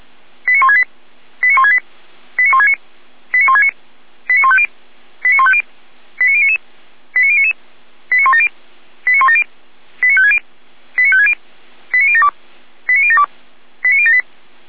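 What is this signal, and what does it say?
Five-tone selective call (ZVEI), heard as quick runs of five tones, tones of about 70 ms. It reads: 89168 89168 89169 89169 89150 89150 89099 89099 89170 89170 89560 89560 89072 89072 89078
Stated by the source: ZVEI five-tone alert received by SDR, Feuerwehr Orlamünde.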